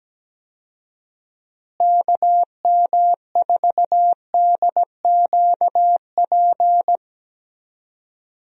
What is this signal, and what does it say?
Morse code, heard as a keyed tone, about 17 wpm, 700 Hz. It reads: KM4DQP
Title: USA Call Signs - 6 Characters - Repeated 3x - 17wpm